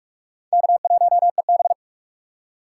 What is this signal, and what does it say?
Morse code, heard as a keyed tone, about 61 wpm, spaced as for 53 wpm, 700 Hz. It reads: K1EB
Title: Speed Racing - USA 1x2 Call Signs 45wpm